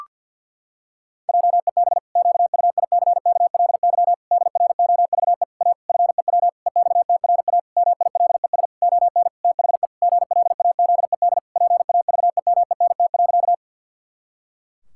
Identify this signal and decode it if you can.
Morse code, heard as a keyed tone, about 52 wpm, 700 Hz. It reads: JEB QUICKLY DROVE A FEW EXTRA MILES ON THE GLAZED PAVEMENT.
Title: Pangram Challenge v2 - 50wpm